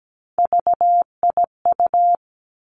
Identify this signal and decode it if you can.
Morse code, heard as a keyed tone, about 17 wpm, 700 Hz. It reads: VIU
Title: Three Challenging Letter-Number Combinations 17wpm